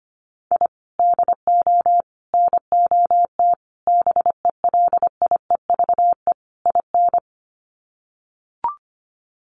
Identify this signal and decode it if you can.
Morse code, heard as a keyed tone, about 25 wpm, 700 Hz. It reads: I DO NOT BELIEVE IN
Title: Sets of 5 Words Encore 25wpm